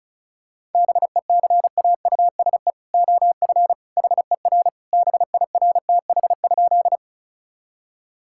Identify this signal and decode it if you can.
Morse code, heard as a keyed tone, about 34 wpm, 700 Hz. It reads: BECAUSE OF HER BIRTH?
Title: Sentences from Top 1000 Words - Review Part 5 - 35wpm